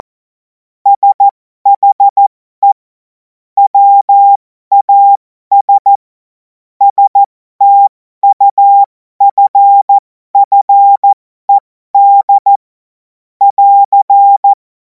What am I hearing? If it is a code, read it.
Morse code, heard as a keyed tone, about 14 wpm, spaced as for 12 wpm, 800 Hz. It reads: SHE WAS STUFFED <AR>
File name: Morsekurs / Witze